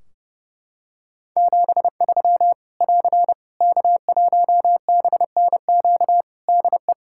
Morse code, heard as a keyed tone, about 30 wpm, 700 Hz. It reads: 73 <AR> K1BNQ DE
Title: Practice QSO - Exchange 4 - Ending - 30wpm